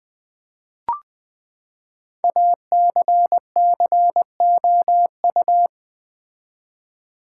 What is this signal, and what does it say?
Morse code, heard as a keyed tone, about 20 wpm, 700 Hz. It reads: ACCOU
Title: Build A Word - 5 Letters At A Time - 10-Letter Words - 20wpm